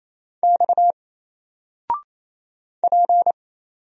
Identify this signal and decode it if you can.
Morse code, heard as a keyed tone, about 27 wpm, 700 Hz. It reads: X P
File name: Single-Letter-Number-28wpm